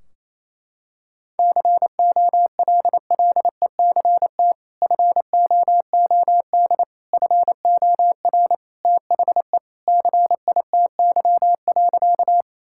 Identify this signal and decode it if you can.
Morse code, heard as a keyed tone, about 28 wpm, 700 Hz. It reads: COLLECT FOOD FOR THE CITY.
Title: Sentences from Top 1600 Words 28wpm